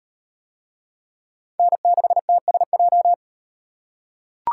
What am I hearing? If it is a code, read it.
Morse code, heard as a keyed tone, about 35 wpm, 700 Hz. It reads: N6TSJ